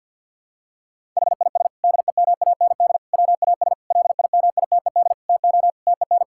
Morse code, heard as a keyed tone, about 47 wpm, 700 Hz. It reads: HIS DEMAND WAS LIMITED TO TEN
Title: Sentences from Top 1000 Words - Review Part 6 - 50wpm